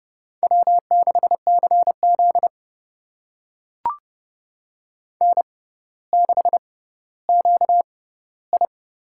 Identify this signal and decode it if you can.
Morse code, heard as a keyed tone, about 30 wpm, spaced as for 15 wpm, 700 Hz. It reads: W6CZ N6QI
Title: USA Call Signs - 4 Characters - 6x Character Spacing - 30wpm